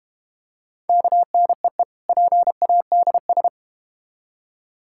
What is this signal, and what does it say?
Morse code, heard as a keyed tone, about 30 wpm, 700 Hz. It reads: KNEE PADS
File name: practice / 25wpm / Difficult Words - Speed-Racing with Hints - 25wpm